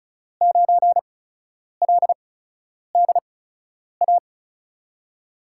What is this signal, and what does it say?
Morse code, heard as a keyed tone, about 35 wpm, 700 Hz. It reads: 9 L D A
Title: USA Call Signs - 6 Characters - 8x Character Spacing - 35wpm